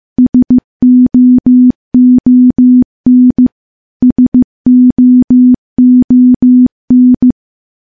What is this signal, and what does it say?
Morse code, heard as a keyed tone, about 16 wpm, 250 Hz. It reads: SOON SOON
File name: morse / morse-training-15WPM